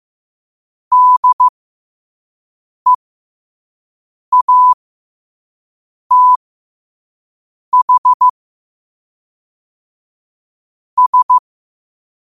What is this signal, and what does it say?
Morse code, heard as a keyed tone, about 15 wpm, spaced as for 5 wpm, 1000 Hz. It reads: DEATH S
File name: qotdfiles / 5wpm